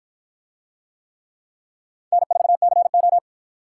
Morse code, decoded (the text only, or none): N4CO